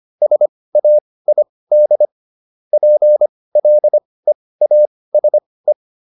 Morse code, heard as a keyed tone, about 25 wpm, spaced as for 18 wpm, 600 Hz. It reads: SAID PLEASE